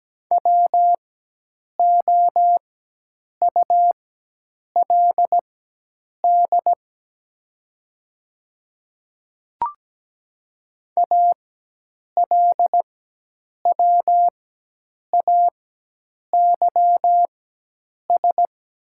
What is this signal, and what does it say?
Morse code, heard as a keyed tone, about 17 wpm, spaced as for 8 wpm, 700 Hz. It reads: WOULD ALWAYS
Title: 4x Character Spacing - Top 500 Words - 17wpm